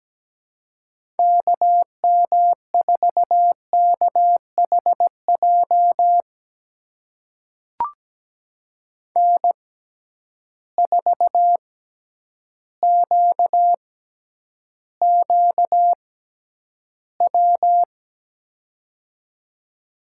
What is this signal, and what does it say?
Morse code, heard as a keyed tone, about 17 wpm, spaced as for 9 wpm, 700 Hz. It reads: KM4KHJ N4QQW
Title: USA Call Signs - 6x Character Spacing - 17wpm